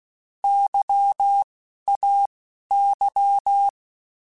Morse code, heard as a keyed tone, about 16 wpm, 800 Hz. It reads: Y A Y
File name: Alphabetical / Alphabetical mp3s / Ys @10